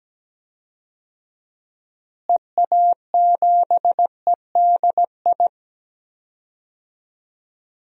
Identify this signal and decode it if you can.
Morse code, heard as a keyed tone, about 17 wpm, 700 Hz. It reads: EA7EDI
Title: DXCC-100 Call Signs - 6 Characters - Repeated 2x - 17wpm